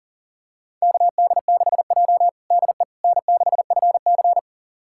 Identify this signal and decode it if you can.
Morse code, heard as a keyed tone, about 40 wpm, 690 Hz. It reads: KD6J DE N6FC
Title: USA Callsign DE Callsign - 4 Characters - 40wpm